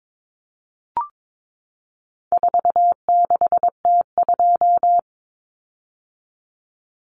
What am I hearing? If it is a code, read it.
Morse code, heard as a keyed tone, about 22 wpm, 700 Hz. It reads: 46T2
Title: Common Cut Numbers - 4-Digit 22wpm